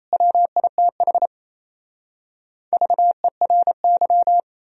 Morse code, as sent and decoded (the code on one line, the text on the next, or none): .-- .. - .... / ...- . .-. -.--
WITH VERY